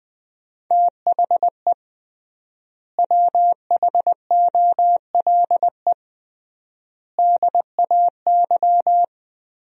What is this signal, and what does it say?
Morse code, heard as a keyed tone, about 20 wpm, 700 Hz. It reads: THE WHOLE DAY
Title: 3x Word-Spacing - Sentences from Top 500 Words - 20wpm